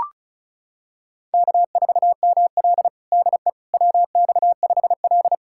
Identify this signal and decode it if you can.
Morse code, heard as a keyed tone, about 36 wpm, 700 Hz. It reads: K4ML DE WX5L
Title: USA Callsign DE Callsign - 4 Characters - 35wpm